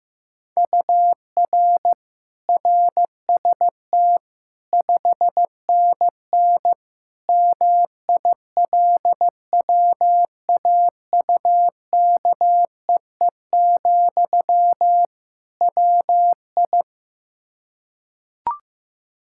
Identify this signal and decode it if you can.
Morse code, heard as a keyed tone, about 15 wpm, 700 Hz. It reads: UR RST 5NN MILWAUKEE, WI
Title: K1USN SST QSO Practice 15wpm